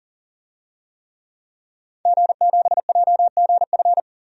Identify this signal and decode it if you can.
Morse code, heard as a keyed tone, about 40 wpm, 700 Hz. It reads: G7JGF
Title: DXCC-100 Call Signs - 5 Characters - 40wpm